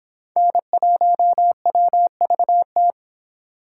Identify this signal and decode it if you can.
Morse code, heard as a keyed tone, about 24 wpm, 700 Hz. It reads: N1WVT